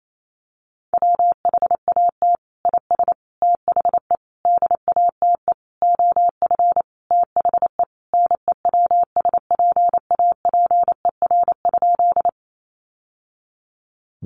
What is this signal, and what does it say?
Morse code, heard as a keyed tone, about 28 wpm, 700 Hz. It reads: WHAT IS THE DATE OF THE NEWSPAPER?
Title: Sentences from Top 1100 Words 28wpm